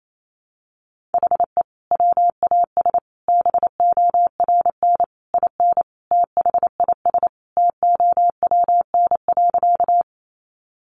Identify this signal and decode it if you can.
Morse code, heard as a keyed tone, about 28 wpm, 700 Hz. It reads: HE WAS BORN IN THIS TOWN.
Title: Sentences from Top 1000 Words 28wpm